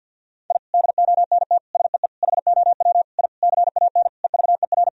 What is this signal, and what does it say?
Morse code, heard as a keyed tone, about 51 wpm, 700 Hz. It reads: I DONT SEE HOW I CAN EVER
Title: Sentences from Top 1000 Words - Review Part 2 - 50wpm